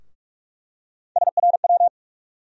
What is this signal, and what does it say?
Morse code, heard as a keyed tone, about 43 wpm, 700 Hz. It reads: SRW
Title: USA Call Sign Postfixes - 3 Letters 45wpm